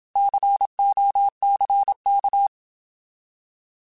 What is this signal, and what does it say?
Morse code, heard as a keyed tone, about 27 wpm, 800 Hz. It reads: COCK
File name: Alphabetical / Alphabetical mp3s / Cs @25